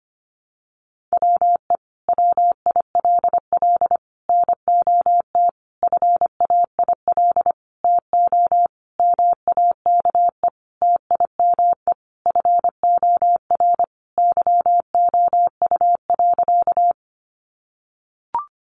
Morse code, heard as a keyed tone, about 25 wpm, 700 Hz. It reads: WE WILL NOT FAIL TO MAKE TIME FOR YOU.
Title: Sentences from Top 1100 Words 25wpm